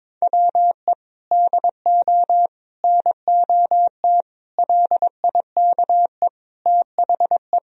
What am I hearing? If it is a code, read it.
Morse code, heard as a keyed tone, about 21 wpm, 700 Hz. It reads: WE DO NOT LIKE THE
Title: Sentences from Top 1000 Words - Review Part 4 - 22wpm